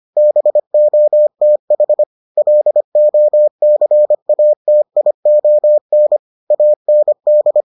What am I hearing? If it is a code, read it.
Morse code, heard as a keyed tone, about 25 wpm, 600 Hz. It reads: BOTH LOCATION AND